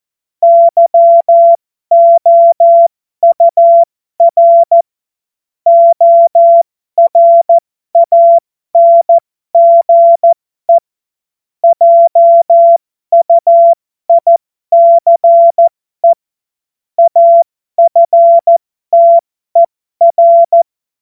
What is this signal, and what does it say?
Morse code, heard as a keyed tone, about 14 wpm, spaced as for 12 wpm, 680 Hz. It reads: YOUR ORANGE JUICE AFTER
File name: Morsekurs / Witze